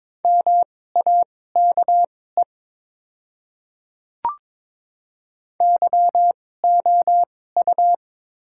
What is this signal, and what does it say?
Morse code, heard as a keyed tone, about 22 wpm, spaced as for 16 wpm, 700 Hz. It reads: MAKE YOU
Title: 2x Character Spacing - Top 100 Words - 22wpm